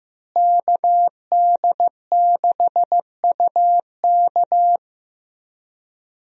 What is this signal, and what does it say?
Morse code, heard as a keyed tone, about 15 wpm, 700 Hz. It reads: KD6UK
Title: USA Call Signs - 5 Characters - Repeated 2x - 15wpm